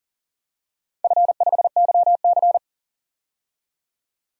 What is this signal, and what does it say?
Morse code, heard as a keyed tone, about 38 wpm, 700 Hz. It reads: F5YC
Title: DXCC-100 Call Signs - 4 Characters - 40wpm